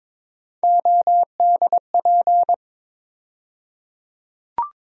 Morse code, heard as a keyed tone, about 21 wpm, 700 Hz. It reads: ODP
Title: USA Call Sign Postfixes - 3 Letters 22wpm